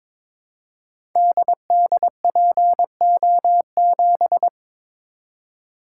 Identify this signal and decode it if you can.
Morse code, heard as a keyed tone, about 22 wpm, 700 Hz. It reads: DDPO7
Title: Code Groups - 5 Letter:Numbers - 22wpm